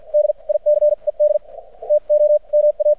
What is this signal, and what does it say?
Morse code, heard as a keyed tone, about 53 wpm, spaced as for 41 wpm, 600 Hz, with radio noise. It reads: DICED TO MA
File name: archived / CRSnet